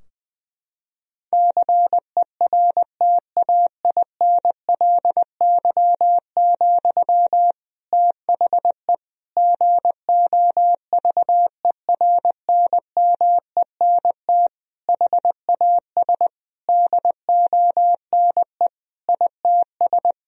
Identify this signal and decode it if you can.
Morse code, heard as a keyed tone, about 21 wpm, 700 Hz. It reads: CERTAINLY, THE GOVERNMENT HAS DONE ITS
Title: Sentences from Top 1000 Words - Review Part 3 - 20wpm